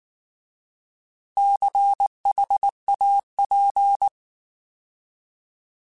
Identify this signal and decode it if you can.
Morse code, heard as a keyed tone, about 19 wpm, 800 Hz. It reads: CHAP